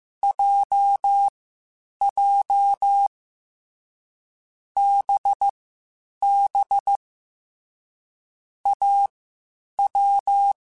Morse code, heard as a keyed tone, about 15 wpm, spaced as for 8 wpm, 800 Hz. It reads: JJ BB AW
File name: Short Runs / Short Runs-mp3s / Unballanced @13